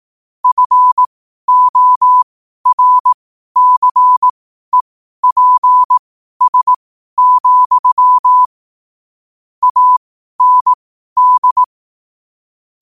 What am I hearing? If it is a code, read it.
Morse code, heard as a keyed tone, about 18 wpm, spaced as for 12 wpm, 1000 Hz. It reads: FORCEPS, AND
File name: qotdfiles / 12wpm